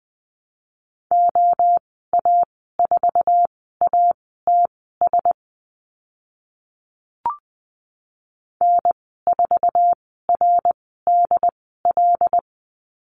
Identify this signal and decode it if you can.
Morse code, heard as a keyed tone, about 20 wpm, spaced as for 14 wpm, 700 Hz. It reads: OA4ATS N4RDL
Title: 082 Lesson 14g - Number 2 - Call Signs 20wpm